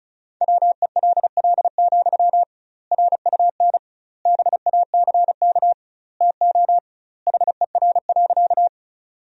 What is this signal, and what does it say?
Morse code, heard as a keyed tone, about 36 wpm, 700 Hz. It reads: WELL, RUN BACK TO HER.
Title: 2x Word-Spacing - Sentences from Top 300 Words - 35wpm